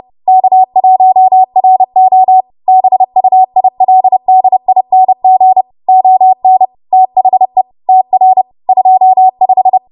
Morse code, heard as a keyed tone, about 30 wpm, 750 Hz. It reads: K1RO BUILDING ON THE TR 25